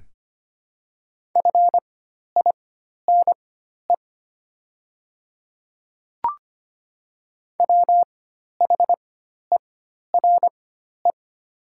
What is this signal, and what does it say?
Morse code, heard as a keyed tone, about 25 wpm, spaced as for 12 wpm, 700 Hz. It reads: FINE WHERE
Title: 4x Character Spacing - Top 500 Words - 25wpm